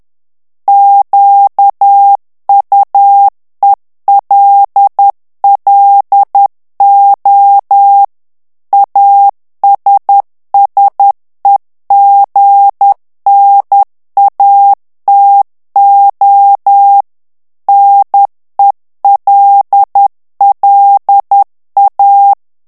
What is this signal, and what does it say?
Morse code, heard as a keyed tone, about 10 wpm, 800 Hz. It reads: QUELLO ASSEGNATO NELLA
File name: ITALIEN texte clair 2 vitesse 10 mots minute